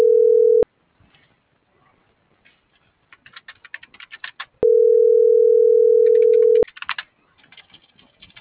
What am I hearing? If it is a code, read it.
Ringback tone (North America), 440 + 480 Hz.